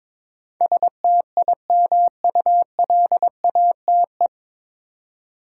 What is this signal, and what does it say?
Morse code, heard as a keyed tone, about 22 wpm, 700 Hz. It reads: STIMULATE
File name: 9 Letter Words - First 3 Letters Given - 22wpm